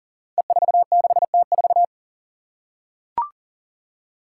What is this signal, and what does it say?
Morse code, heard as a keyed tone, about 40 wpm, 700 Hz. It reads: E46T4
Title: Five Challenging Letter-Number Combinations 40wpm